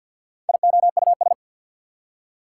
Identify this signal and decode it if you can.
Morse code, heard as a keyed tone, about 65 wpm, spaced as for 57 wpm, 690 Hz. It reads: IOUS
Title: Speed-Racing Prefixes and Suffixes 50wpm